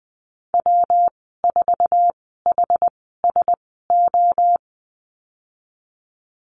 Morse code, heard as a keyed tone, about 20 wpm, spaced as for 15 wpm, 700 Hz. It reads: W4HSO
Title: 128 Lesson 21g - Number 6 - Call Signs 20wpm